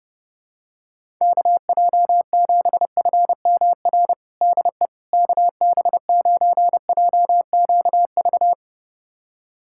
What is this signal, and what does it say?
Morse code, heard as a keyed tone, about 30 wpm, 700 Hz. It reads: KJ7FMR DE KB9JQV
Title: USA Callsign DE Callsign 30wpm